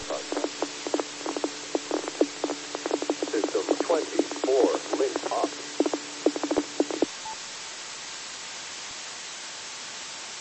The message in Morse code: none

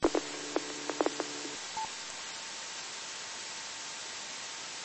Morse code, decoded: none

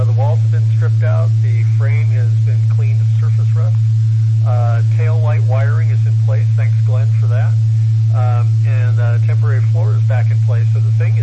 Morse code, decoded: none

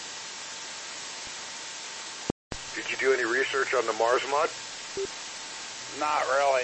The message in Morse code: none

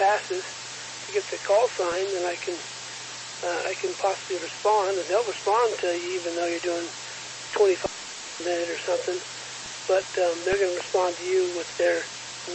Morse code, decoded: none

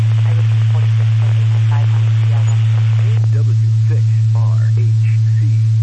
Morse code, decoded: none

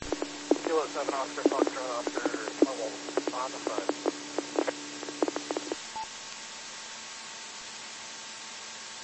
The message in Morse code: none

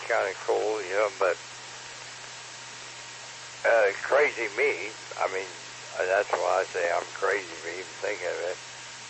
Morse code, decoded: none

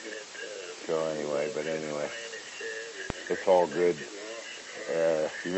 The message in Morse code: none

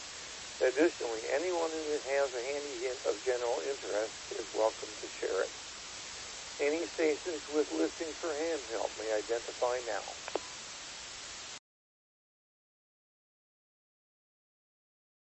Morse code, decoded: none

